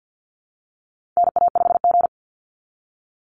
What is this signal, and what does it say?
Morse code, heard as a keyed tone, about 49 wpm, 700 Hz. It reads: NA5G
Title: Call Signs Repeated 3x - 50wpm